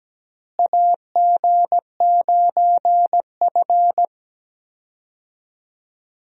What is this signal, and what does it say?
Morse code, heard as a keyed tone, about 17 wpm, 700 Hz. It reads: AG9F
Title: USA Call Signs - 4 Characters - Repeated 3x - 17wpm